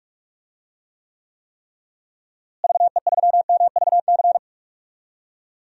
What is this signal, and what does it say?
Morse code, heard as a keyed tone, about 45 wpm, 700 Hz. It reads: VE3MVC
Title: DXCC-100 Call Signs - 6 Characters - Repeated 3x - 45wpm